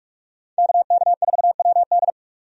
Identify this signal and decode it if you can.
Morse code, heard as a keyed tone, about 46 wpm, 700 Hz. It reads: KK4WD
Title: USA Call Signs - 5 Characters - 45wpm